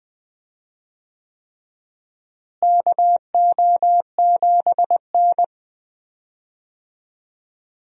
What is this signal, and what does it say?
Morse code, heard as a keyed tone, about 20 wpm, 700 Hz. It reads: KO7N
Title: USA Call Signs - 4 Characters - Repeated 3x - 20wpm